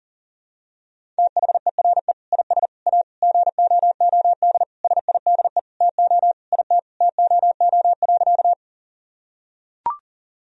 Morse code, heard as a keyed tone, about 40 wpm, 700 Hz. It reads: THERE IS A GOOD SIDE TO IT TOO.
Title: Sentences from Top 1000 Words - Review Part 1 - 40wpm